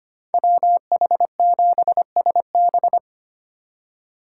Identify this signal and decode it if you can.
Morse code, heard as a keyed tone, about 33 wpm, 700 Hz. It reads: WH7SB